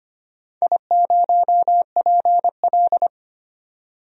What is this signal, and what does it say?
Morse code, heard as a keyed tone, about 25 wpm, 700 Hz. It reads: I0PL
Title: DXCC-100 Call Signs - 4 Characters - 25wpm